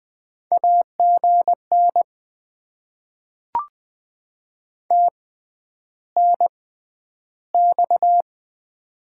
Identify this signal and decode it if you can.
Morse code, heard as a keyed tone, about 20 wpm, spaced as for 10 wpm, 700 Hz. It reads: AGN TNX
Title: Top 90 QSO Elements - 6x Character Spacing - 20wpm